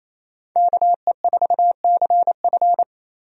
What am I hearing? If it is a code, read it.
Morse code, heard as a keyed tone, about 28 wpm, 700 Hz. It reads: KE4CF
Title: USA Call Signs - 5 Characters - Repeated 2x - 28wpm